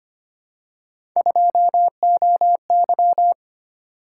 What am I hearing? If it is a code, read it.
Morse code, heard as a keyed tone, about 25 wpm, 700 Hz. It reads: 2OY
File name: Three Letter-Number Combinations - 25wpm